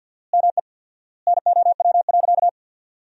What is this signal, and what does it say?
Morse code, heard as a keyed tone, about 47 wpm, 700 Hz. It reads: ME NOW.
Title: 4x Word-Spacing - Sentences from Top 500 Words - 50wpm